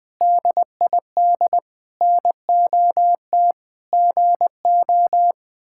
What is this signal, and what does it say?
Morse code, heard as a keyed tone, about 20 wpm, 700 Hz. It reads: DID NOT GO
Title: Sentences with 4-Word Phrases - 20wpm